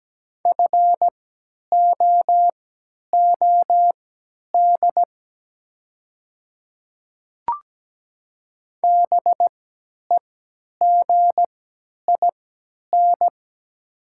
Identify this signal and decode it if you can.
Morse code, heard as a keyed tone, about 17 wpm, spaced as for 10 wpm, 700 Hz. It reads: FOOD BEGIN
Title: 3x Character Spacing - Top 300 Words - 17wpm